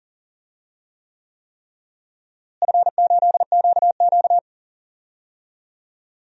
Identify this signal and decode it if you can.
Morse code, heard as a keyed tone, about 40 wpm, 700 Hz. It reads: F8QQ
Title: DXCC-100 Call Signs - 4 Characters - Repeated 3x - 40wpm